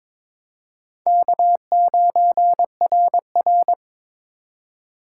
Morse code, heard as a keyed tone, about 26 wpm, 700 Hz. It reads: K9RR
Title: Speed Racing - USA 1x2 Call Signs 22wpm